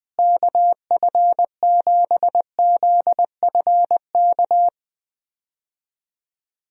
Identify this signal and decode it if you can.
Morse code, heard as a keyed tone, about 20 wpm, 700 Hz. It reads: KF7ZFK
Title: USA Call Signs - 6 Characters - Repeated 3x - 20wpm